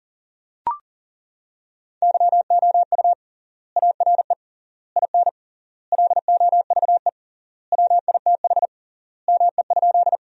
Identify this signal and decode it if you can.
Morse code, heard as a keyed tone, about 37 wpm, 700 Hz. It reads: YOU ARE IN LOVE WITH ME?
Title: 3x Word-Spacing - Sentences from Top 400 Words - 40wpm